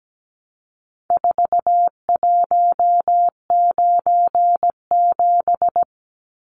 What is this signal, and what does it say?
Morse code, heard as a keyed tone, about 17 wpm, 700 Hz. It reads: 4197
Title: Numbers - 4-Digit 17wpm